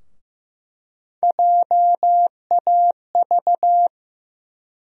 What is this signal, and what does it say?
Morse code, heard as a keyed tone, about 15 wpm, 700 Hz. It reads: JAV
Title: USA Call Sign Postfixes - 3 Letters 15wpm